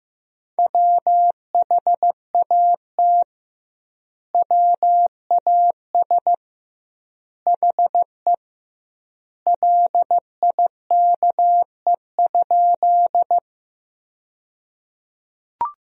Morse code, heard as a keyed tone, about 15 wpm, 700 Hz. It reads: WHAT WAS HE LIKE?
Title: 2x Word-Spacing - Sentences from Top 100 Words - 15wpm